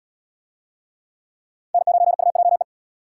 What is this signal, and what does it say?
Morse code, heard as a keyed tone, about 77 wpm, 700 Hz. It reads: N9UJE